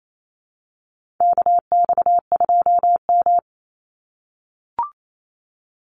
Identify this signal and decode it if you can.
Morse code, heard as a keyed tone, about 28 wpm, 700 Hz. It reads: KX2M
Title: USA Call Signs 28wpm